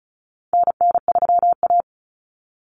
Morse code, heard as a keyed tone, about 34 wpm, 700 Hz. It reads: NN3A
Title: Common Cut Numbers - 4-Digit 35wpm